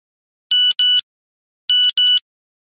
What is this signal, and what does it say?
There is no code.